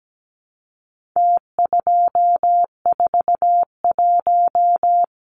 Morse code, heard as a keyed tone, about 17 wpm, 700 Hz. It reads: T241
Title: Numbers - Contest Style - 4-Digit 17wpm